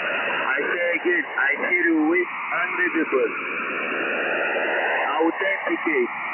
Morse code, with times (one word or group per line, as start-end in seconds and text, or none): none